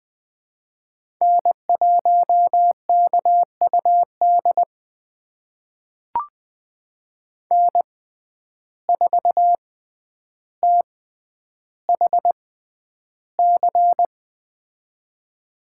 Morse code, heard as a keyed tone, about 20 wpm, spaced as for 10 wpm, 700 Hz. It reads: N1KUD N4THC
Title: USA Call Signs - 5 Characters - 6x Character Spacing - 20wpm